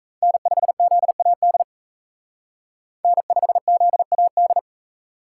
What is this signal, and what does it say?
Morse code, heard as a keyed tone, about 35 wpm, 700 Hz. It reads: N5ZAD N5ZAD
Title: Speed Racing - USA 1x3 Call Signs 28wpm